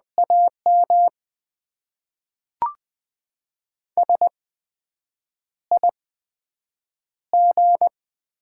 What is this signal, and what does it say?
Morse code, heard as a keyed tone, about 20 wpm, spaced as for 7 wpm, 700 Hz. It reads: AM SIG
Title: Top 90 QSO Elements - 8x Character Spacing - 20wpm